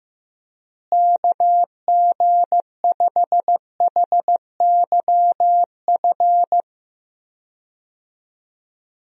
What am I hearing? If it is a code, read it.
Morse code, heard as a keyed tone, about 15 wpm, 700 Hz. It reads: KG5HYF
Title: USA Call Signs - 6 Characters - Repeated 3x - 15wpm